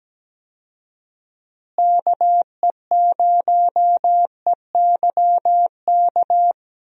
Morse code, heard as a keyed tone, about 17 wpm, 700 Hz. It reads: KE0EYK
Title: USA Call Signs - 6 Characters - Repeated 3x - 17wpm